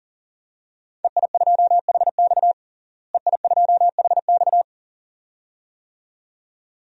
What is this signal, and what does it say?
Morse code, heard as a keyed tone, about 40 wpm, 700 Hz. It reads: EI2HX EI2HX
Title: DXCC-100 Call Signs - 5 Characters - Repeated 2x - 40wpm